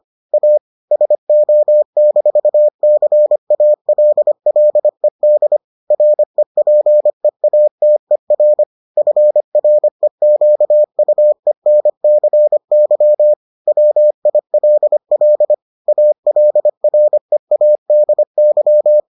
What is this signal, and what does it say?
Morse code, heard as a keyed tone, about 25 wpm, 600 Hz. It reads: A SO-CALLED REPEATER FREQUENCY WILL ALREADY